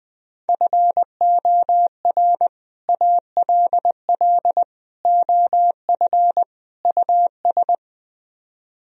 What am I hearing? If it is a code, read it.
Morse code, heard as a keyed tone, about 20 wpm, 700 Hz. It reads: FOR ALL OF US
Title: Speed-Racing - Sets of 4 Words 17wpm